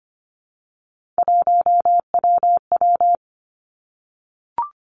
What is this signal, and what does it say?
Morse code, heard as a keyed tone, about 25 wpm, 700 Hz. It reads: 1WW
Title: Three Challenging Letter-Number Combinations 25wpm